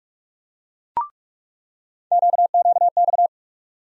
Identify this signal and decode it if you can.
Morse code, heard as a keyed tone, about 46 wpm, 700 Hz. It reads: QQX
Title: USA Call Sign Postfixes - 3 Letters 45wpm